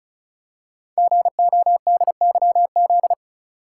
Morse code, heard as a keyed tone, about 36 wpm, 700 Hz. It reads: GODYZ
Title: Code Groups - 5 Letters - 35wpm